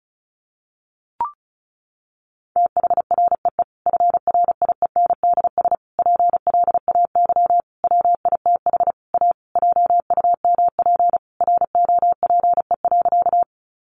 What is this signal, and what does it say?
Morse code, heard as a keyed tone, about 34 wpm, 700 Hz. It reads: THREE FRIENDS PLAY WITH A JUMP ROPE.
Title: Easy Phrases 35wpm